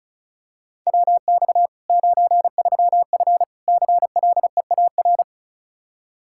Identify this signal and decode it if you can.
Morse code, heard as a keyed tone, about 36 wpm, 700 Hz. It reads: WX 93F CLEAR
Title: Practice QSO - Exchange 2 - Info - 35wpm